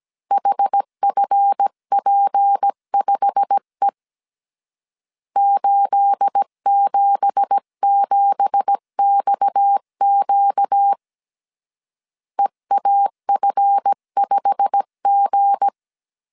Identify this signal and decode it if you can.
Morse code, heard as a keyed tone, about 17 wpm, 800 Hz. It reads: HFP5E 877XQ EAF5G
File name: 1500C S B 16wpm